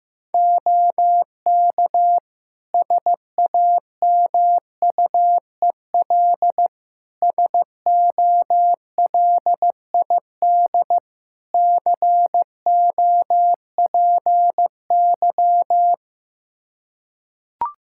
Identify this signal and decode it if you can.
Morse code, heard as a keyed tone, about 15 wpm, 700 Hz. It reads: OK SAMUEL SOLID COPY
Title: K1USN SST QSO Practice 15wpm